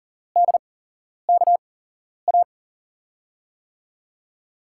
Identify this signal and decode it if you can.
Morse code, heard as a keyed tone, about 38 wpm, spaced as for 11 wpm, 700 Hz. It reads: DKA
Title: USA Call Signs - 6 Characters - 8x Character Spacing - 40wpm